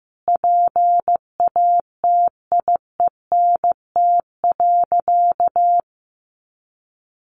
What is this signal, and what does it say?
Morse code, heard as a keyed tone, about 15 wpm, 700 Hz. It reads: PATIENT.
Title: Additional Easy Phrases 15wpm